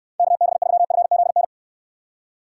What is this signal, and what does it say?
Morse code, heard as a keyed tone, about 69 wpm, 700 Hz. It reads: KB3FZN